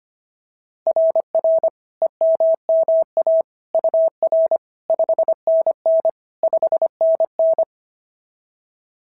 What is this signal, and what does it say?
Morse code, heard as a keyed tone, about 25 wpm, 660 Hz. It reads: RR EMMA UR 5NN 5NN
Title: SOTA QSO Practice 25wpm